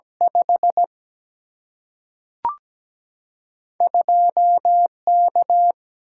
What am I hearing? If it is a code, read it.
Morse code, heard as a keyed tone, about 17 wpm, 700 Hz. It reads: 5 2 K
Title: Single-Letter Single-Number 17wpm